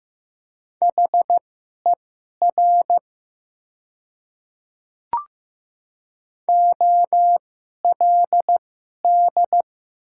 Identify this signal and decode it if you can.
Morse code, heard as a keyed tone, about 15 wpm, spaced as for 11 wpm, 700 Hz. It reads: HER OLD
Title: 2x Character Spacing - Top 300 Words - 15wpm